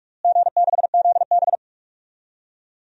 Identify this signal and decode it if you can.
Morse code, heard as a keyed tone, about 46 wpm, 690 Hz. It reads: G6ZB
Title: DXCC-100 Call Signs - 4 Characters - Repeated 3x - 45wpm